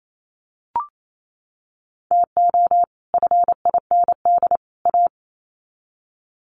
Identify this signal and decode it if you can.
Morse code, heard as a keyed tone, about 28 wpm, 700 Hz. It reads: TO FIND A
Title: Sets of 3 Words 28wpm